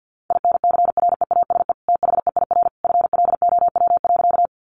Morse code, heard as a keyed tone, about 50 wpm, 700 Hz. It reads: INCREASE THEIR FLOW.
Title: Sentences from Top 1200 Words 50wpm